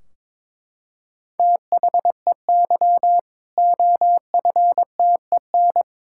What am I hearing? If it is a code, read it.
Morse code, heard as a keyed tone, about 22 wpm, 700 Hz. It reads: THEY OFTEN
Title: Sentences from Top 1600 Words 22wpm